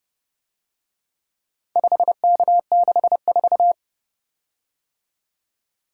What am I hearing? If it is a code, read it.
Morse code, heard as a keyed tone, about 30 wpm, 700 Hz. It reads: 5K64